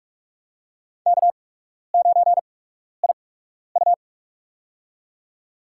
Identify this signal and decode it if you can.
Morse code, heard as a keyed tone, about 44 wpm, spaced as for 12 wpm, 700 Hz. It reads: K9IU
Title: USA Call Signs - 4 Characters - 8x Character Spacing - 45wpm